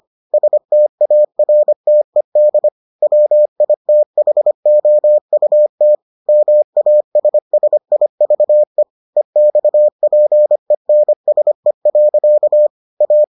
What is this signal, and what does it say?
Morse code, heard as a keyed tone, about 25 wpm, 600 Hz. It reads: STARTED WITHOUT MASSIVE EXPENSE. A